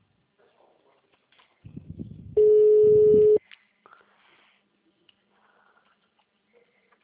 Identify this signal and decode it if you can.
Ringback tone (Europe, 425 Hz).